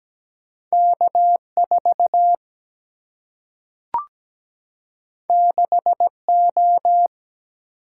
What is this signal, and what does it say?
Morse code, heard as a keyed tone, about 17 wpm, 700 Hz. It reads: K4 6O
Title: Single-Letter Single-Number 17wpm